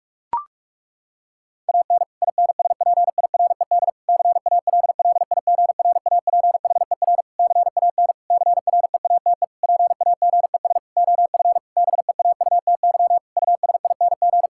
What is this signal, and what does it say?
Morse code, heard as a keyed tone, about 46 wpm, 700 Hz. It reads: AN INSPIRED CALLIGRAPHER CAN CREATE PAGES OF BEAUTY USING
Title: Pangram Challenge v2 - 45wpm